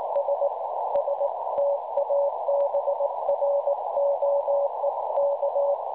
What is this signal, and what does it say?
Morse code, heard as a keyed tone, about 19 wpm, 570 Hz, with noise. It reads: S S TABROEK
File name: cwronde / websdr recording .6kHz